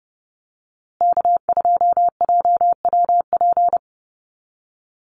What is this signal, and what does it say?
Morse code, heard as a keyed tone, about 29 wpm, 700 Hz. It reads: K2JWP
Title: USA Call Signs Repeated 3x 30wpm